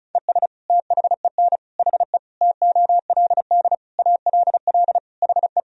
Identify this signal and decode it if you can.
Morse code, heard as a keyed tone, about 36 wpm, 700 Hz. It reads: ES THEN HE TOLD ALL HE
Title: Sentences from Top 400 Words and Abbreviations 35wpm